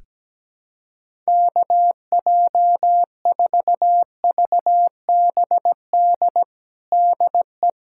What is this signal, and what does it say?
Morse code, heard as a keyed tone, about 17 wpm, 700 Hz. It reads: KJ4VBD DE
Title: USA Callsign DE Callsign 17wpm